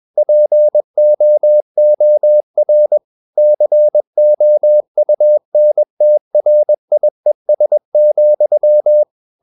Morse code, heard as a keyed tone, about 21 wpm, 600 Hz. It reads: POOR COUNTRIES,
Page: CW med Gnister